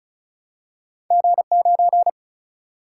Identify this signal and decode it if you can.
Morse code, heard as a keyed tone, about 33 wpm, 700 Hz. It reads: G 9
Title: Single-Letter Single-Number 35wpm